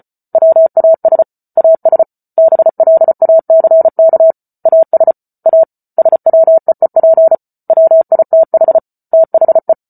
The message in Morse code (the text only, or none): WAS AS BLACK AS A SWEEP WITH THE